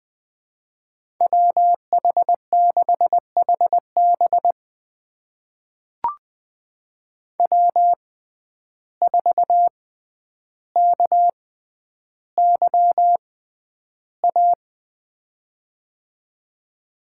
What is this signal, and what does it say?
Morse code, heard as a keyed tone, about 20 wpm, spaced as for 10 wpm, 700 Hz. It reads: WH6HB W4KYA
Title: USA Call Signs - 5 Characters - 6x Character Spacing - 20wpm